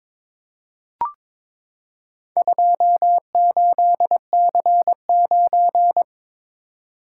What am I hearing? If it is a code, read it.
Morse code, heard as a keyed tone, about 22 wpm, 700 Hz. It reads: 28C9
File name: practice / 22wpm / Four Letter-Number Combinations - 22wpm